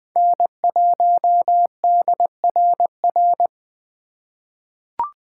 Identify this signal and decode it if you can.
Morse code, heard as a keyed tone, about 20 wpm, 700 Hz. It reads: N1DRR
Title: USA Call Signs - 3x Character Spacing - 20wpm